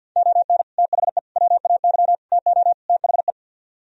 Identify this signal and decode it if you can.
Morse code, heard as a keyed tone, about 63 wpm, 700 Hz. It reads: ON THE WAY TO THE